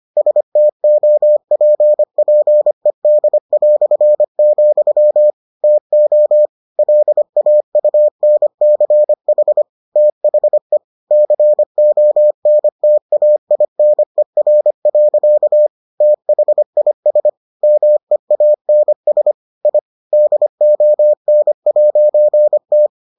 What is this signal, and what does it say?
Morse code, heard as a keyed tone, about 25 wpm, 600 Hz. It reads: STOPPED", TO LAUNCH THE CONTAINER. THIS MEANS I DON'T